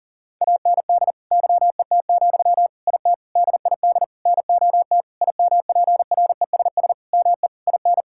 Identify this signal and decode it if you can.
Morse code, heard as a keyed tone, about 40 wpm, 700 Hz. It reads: AND YET, IT DID NOT IMPRESS ME IN
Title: Sentences from Top 1600 Words 40wpm